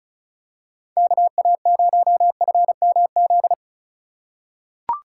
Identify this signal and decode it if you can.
Morse code, heard as a keyed tone, about 36 wpm, 700 Hz. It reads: KA0FMZ
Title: USA Call Signs - 6 Characters - 4x Character Spacing - 35wpm